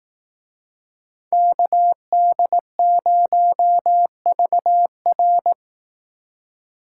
Morse code, heard as a keyed tone, about 19 wpm, 700 Hz. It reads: KD0VR